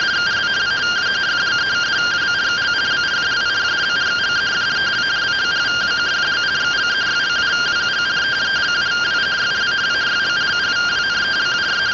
Two-tone teleprinter signal, about 170 Hz shift.